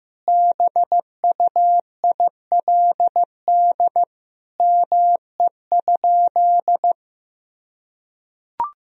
Morse code, heard as a keyed tone, about 15 wpm, 700 Hz. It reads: BUILD ME?
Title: Sentences from Top 1000 Words - Review Part 1 - 15wpm